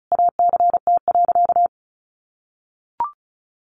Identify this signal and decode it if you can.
Morse code, heard as a keyed tone, about 34 wpm, 700 Hz. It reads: ACT.
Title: Sentences from Top 1300 Words 35wpm